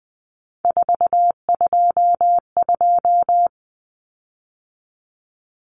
Morse code, heard as a keyed tone, about 20 wpm, 700 Hz. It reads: 422
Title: Common Cut Numbers - 3-Digit 20wpm